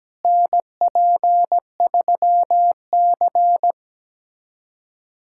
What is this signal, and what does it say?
Morse code, heard as a keyed tone, about 17 wpm, 700 Hz. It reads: NP3C
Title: USA Callsign DE Callsign - 4 Characters - 17wpm